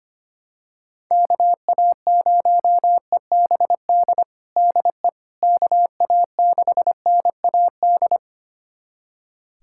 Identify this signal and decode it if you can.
Morse code, heard as a keyed tone, about 25 wpm, 700 Hz. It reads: KA0EBD DE KA6NAD
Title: USA Callsign DE Callsign - 6 Characters - 25wpm